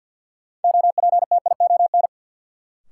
Morse code, heard as a keyed tone, about 52 wpm, 700 Hz. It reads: OPTION
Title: Sentences from Top 1300 Words - New Word Drill Alt - 50wpm